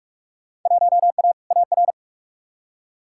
Morse code, heard as a keyed tone, about 45 wpm, 700 Hz. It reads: 1A AR
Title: Field Day Exchanges Encore - 45wpm